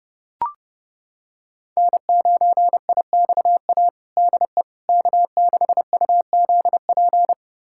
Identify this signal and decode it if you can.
Morse code, heard as a keyed tone, about 30 wpm, 700 Hz. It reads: N9IXA DE K6UZP
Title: USA Callsign DE Callsign - 5 Characters - 30wpm